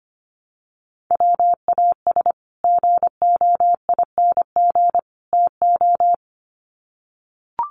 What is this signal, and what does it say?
Morse code, heard as a keyed tone, about 25 wpm, 700 Hz. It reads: WAS GOING TO
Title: Sets of 3 Words 25wpm